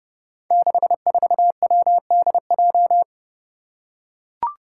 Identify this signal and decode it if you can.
Morse code, heard as a keyed tone, about 31 wpm, 700 Hz. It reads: 64WDJ
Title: Five Challenging Letter-Number Combinations 30wpm